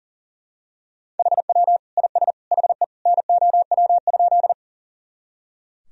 Morse code, heard as a keyed tone, about 40 wpm, 700 Hz. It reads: HW IS HE NOW?
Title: Sentences from Top 100 Words and Abbreviations 40wpm